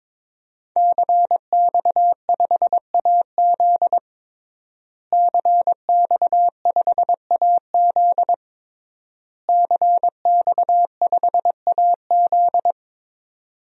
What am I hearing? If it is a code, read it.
Morse code, heard as a keyed tone, about 23 wpm, 700 Hz. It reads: CX5AZ CX5AZ CX5AZ
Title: DXCC-100 Call Signs - 5 Characters - Repeated 3x - 22wpm